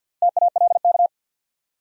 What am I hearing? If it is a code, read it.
Morse code, heard as a keyed tone, about 65 wpm, 700 Hz. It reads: TALK